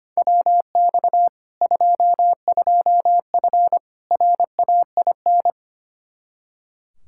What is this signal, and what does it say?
Morse code, heard as a keyed tone, about 25 wpm, 700 Hz. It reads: WX 22F RAIN
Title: Practice QSO - Exchange 2 - Info - 25wpm